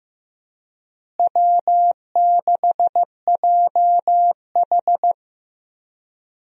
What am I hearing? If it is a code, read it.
Morse code, heard as a keyed tone, about 16 wpm, 700 Hz. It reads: W6JH
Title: USA Call Signs - 4 Characters - Repeated 3x - 15wpm